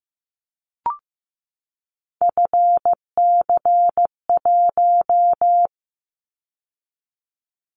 Morse code, heard as a keyed tone, about 15 wpm, 700 Hz. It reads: FC1
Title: Callsign Prefixes for Top 50 DXCC - 15wpm